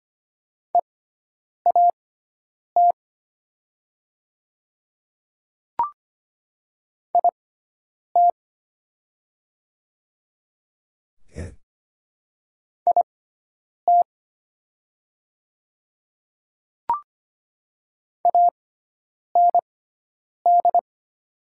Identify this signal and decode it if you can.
Morse code, heard as a keyed tone, about 25 wpm, spaced as for 9 wpm, 700 Hz. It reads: EAT IT IT AND
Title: 6x Character Spacing - Top 300 Words - 25wpm